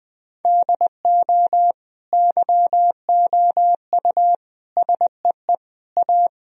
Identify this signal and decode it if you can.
Morse code, heard as a keyed tone, about 20 wpm, 700 Hz. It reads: DO YOU SEE A
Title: Sentences from Top 1000 Words - Review Part 4 - 20wpm